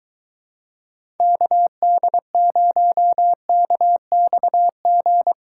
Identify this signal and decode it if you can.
Morse code, heard as a keyed tone, about 22 wpm, 700 Hz. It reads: KD0KXG